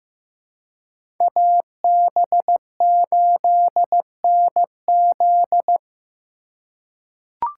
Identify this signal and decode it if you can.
Morse code, heard as a keyed tone, about 15 wpm, 700 Hz. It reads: AB8NZ
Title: USA Call Signs - 5 Characters - 3x Character Spacing - 15wpm